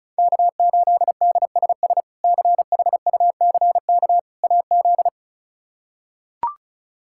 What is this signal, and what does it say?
Morse code, heard as a keyed tone, about 36 wpm, 700 Hz. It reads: K8DSS CHUCK AZ
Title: K1USN SST QSO Practice - 35wpm